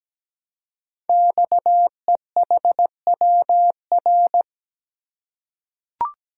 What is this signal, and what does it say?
Morse code, heard as a keyed tone, about 17 wpm, 700 Hz. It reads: XEHWR
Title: Code Groups - 5 Letters - 17wpm